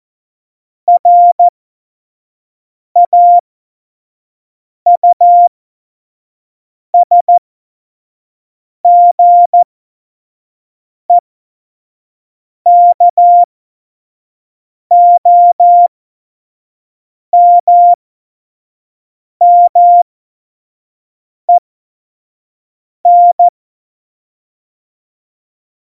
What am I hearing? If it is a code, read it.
Morse code, heard as a keyed tone, about 14 wpm, spaced as for 5 wpm, 700 Hz. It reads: RAUSGEKOMMEN
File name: Morsekurs / Witze / Witz 12. September 17, Tempo 25